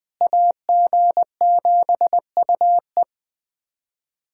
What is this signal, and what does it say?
Morse code, heard as a keyed tone, about 20 wpm, 700 Hz. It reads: AG7UE
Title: USA Callsign DE Callsign - 5 Characters - 20wpm